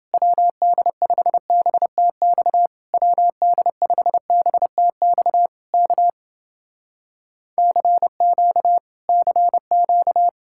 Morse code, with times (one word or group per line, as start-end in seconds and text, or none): 0.1-2.7 WD5BTX
2.9-5.5 WD5BTX
5.7-6.1 K
7.6-8.8 CQ
9.1-10.3 CQ